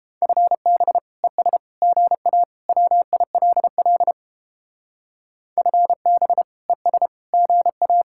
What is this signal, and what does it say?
Morse code, heard as a keyed tone, about 29 wpm, 700 Hz. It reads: FB ES GA WILL FB ES GA